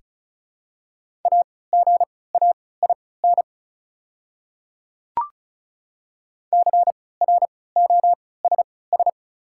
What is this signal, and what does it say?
Morse code, heard as a keyed tone, about 36 wpm, spaced as for 20 wpm, 700 Hz. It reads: AGAIN CROSS
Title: 3x Character Spacing - Top 300 Words - 35wpm